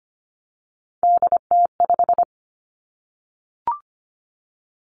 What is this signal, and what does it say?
Morse code, heard as a keyed tone, about 25 wpm, 700 Hz. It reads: DT5
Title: Three Challenging Letter-Number Combinations 25wpm